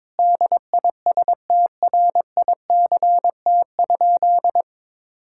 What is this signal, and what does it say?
Morse code, heard as a keyed tone, about 22 wpm, 700 Hz. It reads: DISTRICT?
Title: Sentences from Top 1600 Words 22wpm